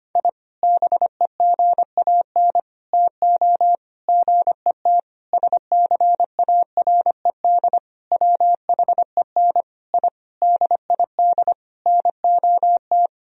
Morse code, heard as a keyed tone, about 25 wpm, 700 Hz. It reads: I BEGAN TO GET SCARED WHEN I DID NOT
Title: Sentences from Top 1000 Words - Review Part 4 - 25wpm